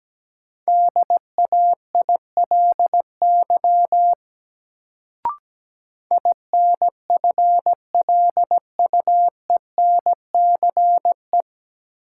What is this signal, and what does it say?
Morse code, heard as a keyed tone, about 17 wpm, 700 Hz. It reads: DAILY INFLUENCE
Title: Sentences from Top 1100 Words - New Word Drill - 17wpm